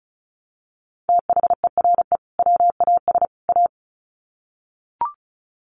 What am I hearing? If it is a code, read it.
Morse code, heard as a keyed tone, about 34 wpm, 700 Hz. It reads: THERE WAS A
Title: Sets of 3 Words 35wpm